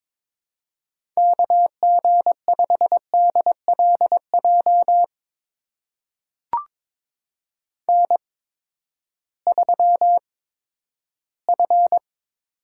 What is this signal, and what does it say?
Morse code, heard as a keyed tone, about 22 wpm, spaced as for 12 wpm, 700 Hz. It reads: KG5DLJ N3F
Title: USA Call Signs - 8x Character Spacing - 22wpm